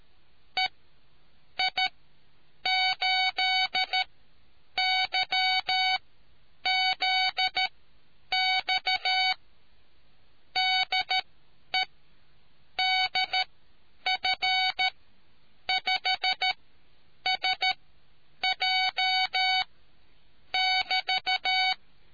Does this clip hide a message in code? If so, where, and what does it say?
Yes: Morse code from 0.6 to 21.7 s, heard as a keyed tone, about 13 wpm, spaced as for 8 wpm, 750 Hz. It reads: EI8YZX DE DF5SJ<BT>